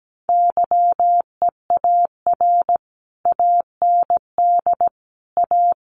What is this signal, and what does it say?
Morse code, heard as a keyed tone, about 17 wpm, 700 Hz. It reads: YEAR AND A
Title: Sets of 5 Words Encore 17wpm